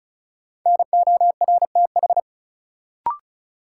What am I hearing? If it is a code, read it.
Morse code, heard as a keyed tone, about 36 wpm, 700 Hz. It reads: NORTH
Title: Sentences from Top 400 Words - New Word Drill - 35wpm